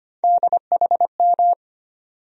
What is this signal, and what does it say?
Morse code, heard as a keyed tone, about 25 wpm, 700 Hz. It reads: DHM
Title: USA Call Sign Postfixes - 3 Letters 25wpm